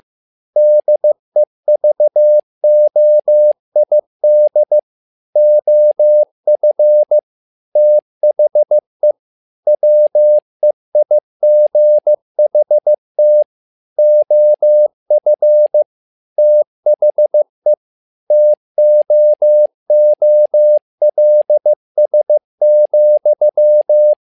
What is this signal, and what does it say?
Morse code, heard as a keyed tone, about 15 wpm, 600 Hz. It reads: DEVOID OF THE WEIGHT OF THE TOOLS,
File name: learning center / tower trouble / tower trouble 15wpm